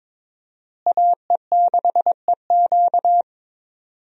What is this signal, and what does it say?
Morse code, heard as a keyed tone, about 22 wpm, 700 Hz. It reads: AE6EQ
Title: USA Call Signs - 5 Characters - Repeated 3x - 22wpm